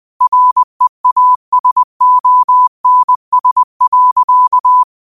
Morse code, heard as a keyed tone, about 20 wpm, 1000 Hz. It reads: REASONS.